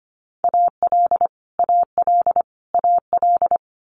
Morse code, heard as a keyed tone, about 25 wpm, 700 Hz. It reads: AL AL AL
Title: Multi-letter Phonemes Repeated 3x 25wpm